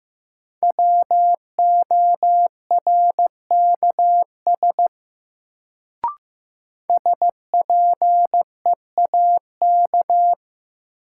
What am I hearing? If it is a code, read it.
Morse code, heard as a keyed tone, about 15 wpm, 700 Hz. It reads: WORKS SPEAK
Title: Sentences from Top 1100 Words - New Word Drill Alt - 15wpm